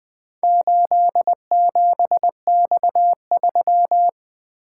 Morse code, heard as a keyed tone, about 20 wpm, 700 Hz. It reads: 87X3
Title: Four Letter-Number Combinations - 20wpm